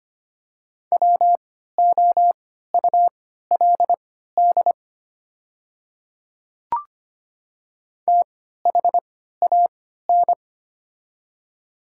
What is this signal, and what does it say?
Morse code, heard as a keyed tone, about 25 wpm, spaced as for 14 wpm, 700 Hz. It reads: WOULD THAN
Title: 3x Character Spacing - Top 100 Words - 25wpm